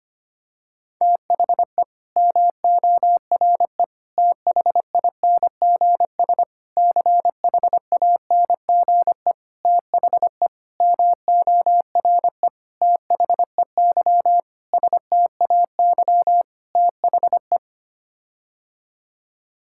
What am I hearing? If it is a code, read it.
Morse code, heard as a keyed tone, about 25 wpm, 700 Hz. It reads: THE MORE THINGS CHANGE THE MORE THEY STAY THE
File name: Sets of 9 Words - 25wpm